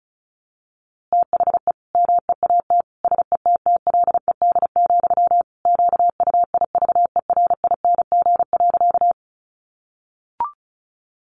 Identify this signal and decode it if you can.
Morse code, heard as a keyed tone, about 34 wpm, 700 Hz. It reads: THE MEAT SETTLED, QUIVERING.
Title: A Sound Of Thunder 35wpm